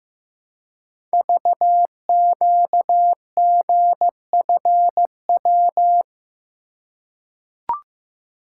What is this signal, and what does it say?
Morse code, heard as a keyed tone, about 15 wpm, 700 Hz. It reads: VQGFW
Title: Code Groups - 5 Letters - 15wpm